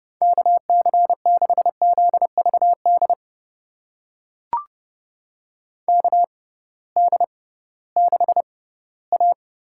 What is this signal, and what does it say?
Morse code, heard as a keyed tone, about 30 wpm, spaced as for 17 wpm, 700 Hz. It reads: KC6ZVD KD6A
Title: USA Call Signs - 6 Characters - 6x Character Spacing - 30wpm